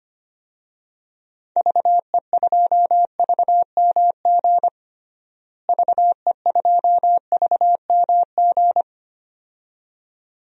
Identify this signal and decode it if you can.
Morse code, heard as a keyed tone, about 25 wpm, 700 Hz. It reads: VE2VMG VE2VMG
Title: DXCC-100 Call Signs - 6 Characters - Repeated 2x - 25wpm